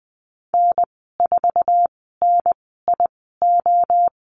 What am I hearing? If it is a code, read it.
Morse code, heard as a keyed tone, about 20 wpm, spaced as for 15 wpm, 700 Hz. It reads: N4NIO
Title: 048 Lesson 9g - Number 4 - Call Signs 20wpm